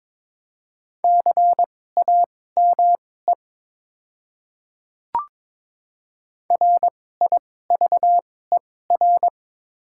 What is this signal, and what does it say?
Morse code, heard as a keyed tone, about 22 wpm, spaced as for 16 wpm, 700 Hz. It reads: CAME RIVER
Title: 2x Character Spacing - Top 300 Words - 22wpm